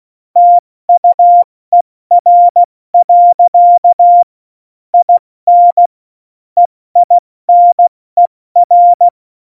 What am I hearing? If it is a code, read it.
Morse code, heard as a keyed tone, about 16 wpm, spaced as for 14 wpm, 700 Hz. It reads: TUER. IN EINER
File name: Morsekurs / Witze / Witz 12 Mai, 2020 70 F80